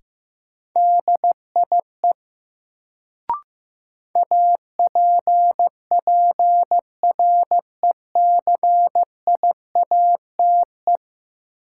Morse code, heard as a keyed tone, about 15 wpm, 700 Hz. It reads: DIE APPRECIATE
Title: Sentences from Top 1300 Words - New Word Drill - 15wpm